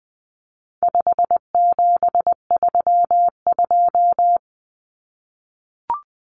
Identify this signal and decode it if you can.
Morse code, heard as a keyed tone, about 20 wpm, 700 Hz. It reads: 5732
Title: Numbers - 4-Digit 20wpm